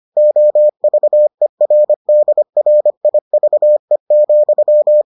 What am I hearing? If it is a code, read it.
Morse code, heard as a keyed tone, about 25 wpm, 600 Hz. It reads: OVERDRIVE,